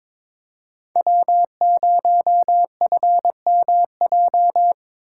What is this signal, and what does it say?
Morse code, heard as a keyed tone, about 25 wpm, 700 Hz. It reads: W0FMJ
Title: Speed Racing - USA 1x3 Call Signs 22wpm